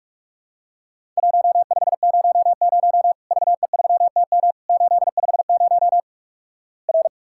Morse code, heard as a keyed tone, about 48 wpm, 700 Hz. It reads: 1500 VE3TM 850 R
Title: Worked All Europe DX Contest - QTC Exchange - 45wpm